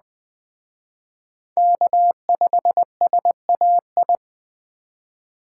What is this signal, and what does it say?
Morse code, heard as a keyed tone, about 20 wpm, 700 Hz. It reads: K5SAI
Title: Speed Racing - USA 1x3 Call Signs 17wpm